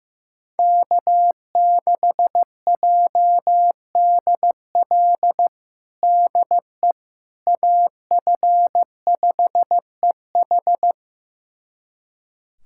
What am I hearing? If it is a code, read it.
Morse code, heard as a keyed tone, about 15 wpm, 700 Hz. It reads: K6JDL DE AF5EH
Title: USA Callsign DE Callsign - 5 Characters - 15wpm